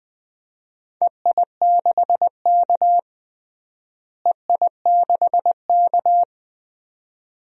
Morse code, heard as a keyed tone, about 20 wpm, 700 Hz. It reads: EI6K EI6K
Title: DXCC-100 Call Signs - 4 Characters - Repeated 2x - 20wpm